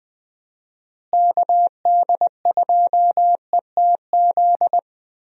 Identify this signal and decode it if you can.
Morse code, heard as a keyed tone, about 20 wpm, 700 Hz. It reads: KD2ETZ
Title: USA Call Signs - 6 Characters - Repeated 3x - 20wpm